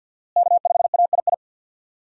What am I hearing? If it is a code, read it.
Morse code, heard as a keyed tone, about 47 wpm, 700 Hz. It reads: K5AII
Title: USA Call Signs - 5 Characters - Repeated 2x - 50wpm